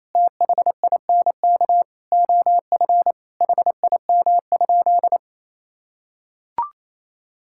Morse code, heard as a keyed tone, about 28 wpm, 700 Hz. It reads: THINK OF HIM?
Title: Sentences from Top 1000 Words - Review Part 5 - 28wpm